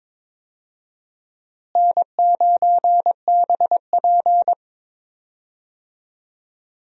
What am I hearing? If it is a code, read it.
Morse code, heard as a keyed tone, about 22 wpm, 700 Hz. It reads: N9BP
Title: USA Call Signs - 4 Characters - Repeated 2x - 22wpm